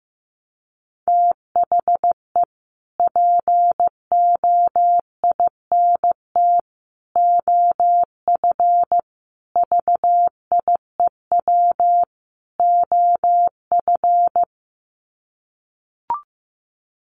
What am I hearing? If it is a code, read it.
Morse code, heard as a keyed tone, about 15 wpm, 700 Hz. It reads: THE POINT OF VIEW OF
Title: Sets of 5 Words 15wpm